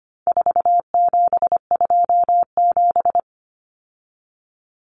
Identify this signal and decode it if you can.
Morse code, heard as a keyed tone, about 25 wpm, 700 Hz. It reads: 4727
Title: Numbers - 4-Digit 25wpm